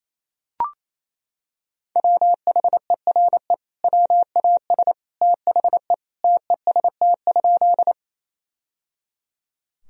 Morse code, heard as a keyed tone, about 28 wpm, 700 Hz. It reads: WHERE WAS THE TEST?
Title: Sentences from Top 1000 Words - Review Part 3 - 28wpm